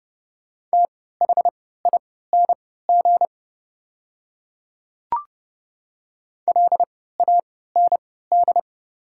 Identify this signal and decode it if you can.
Morse code, heard as a keyed tone, about 30 wpm, spaced as for 17 wpm, 690 Hz. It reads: THING LAND
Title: 3x Character Spacing - Top 200 Words - 30wpm